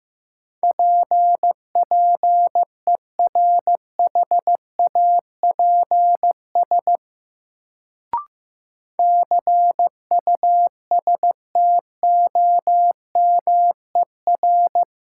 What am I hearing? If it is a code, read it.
Morse code, heard as a keyed tone, about 15 wpm, 700 Hz. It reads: P PERHAPS CUSTOMER
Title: Sentences from Top 1100 Words - New Word Drill - 15wpm